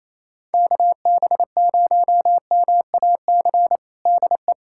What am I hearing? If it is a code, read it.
Morse code, heard as a keyed tone, about 27 wpm, 700 Hz. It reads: KB0MAC DE
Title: USA Callsign DE Callsign - 6 Characters - 28wpm